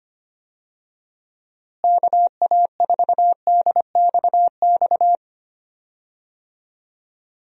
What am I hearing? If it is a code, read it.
Morse code, heard as a keyed tone, about 25 wpm, 700 Hz. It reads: KA4DXX
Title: USA Call Signs - 6 Characters - Repeated 3x - 25wpm